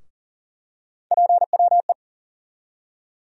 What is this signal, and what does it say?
Morse code, heard as a keyed tone, about 40 wpm, 700 Hz. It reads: PWE
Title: USA Call Sign Postfixes - 3 Letters 40wpm